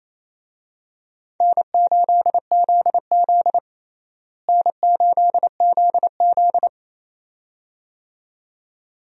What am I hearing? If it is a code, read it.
Morse code, heard as a keyed tone, about 27 wpm, 700 Hz. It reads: N8ZZ N8ZZ
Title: USA Call Signs - 4 Characters - Repeated 2x - 28wpm